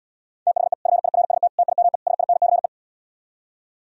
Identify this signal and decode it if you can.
Morse code, heard as a keyed tone, about 77 wpm, 700 Hz. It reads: THE FIRST TIME SINCE